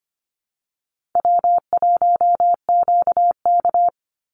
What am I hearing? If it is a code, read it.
Morse code, heard as a keyed tone, about 25 wpm, 700 Hz. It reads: W1QK
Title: Easy Call Signs Repeated 3x - 25wpm